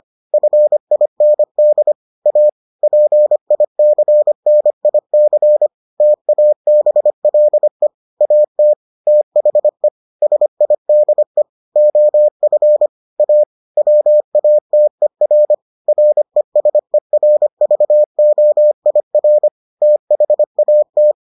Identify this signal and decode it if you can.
Morse code, heard as a keyed tone, about 25 wpm, 600 Hz. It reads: FIND A PICNIC TABLE AT THE SIDE OF A WATER RESERVOIR THAT